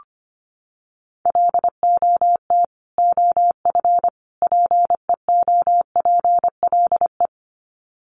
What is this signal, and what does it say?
Morse code, heard as a keyed tone, about 25 wpm, 700 Hz. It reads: LOT OF PEOPLE
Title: Sets of 3 Words 25wpm